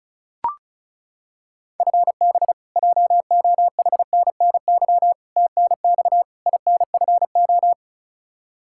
Morse code, heard as a keyed tone, about 34 wpm, 700 Hz. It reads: FB JOHNNY TNX INFO
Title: Practice QSO - Exchange 3 - More Info - 35wpm